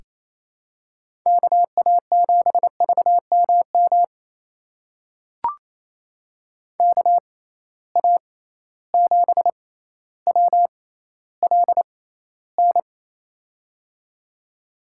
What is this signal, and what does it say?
Morse code, heard as a keyed tone, about 27 wpm, spaced as for 14 wpm, 700 Hz. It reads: KA7VMM KA7WLN
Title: USA Call Signs - 6x Character Spacing - 28wpm